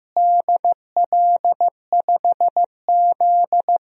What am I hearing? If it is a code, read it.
Morse code, heard as a keyed tone, about 15 wpm, 700 Hz. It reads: DL5Z
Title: DXCC-100 Call Signs - 4 Characters - Repeated 2x - 15wpm